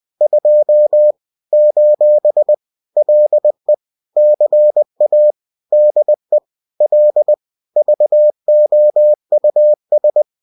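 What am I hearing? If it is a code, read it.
Morse code, heard as a keyed tone, about 20 wpm, 600 Hz. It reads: 2 : LE CA DE L VOUS
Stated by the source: Bulletin sent in CW.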